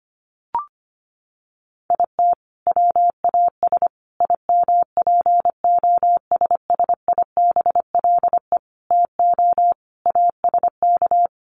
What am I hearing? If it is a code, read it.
Morse code, heard as a keyed tone, about 25 wpm, 700 Hz. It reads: IT WAS IMPOSSIBLE TO ASK
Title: Sentences from Top 1100 Words 25wpm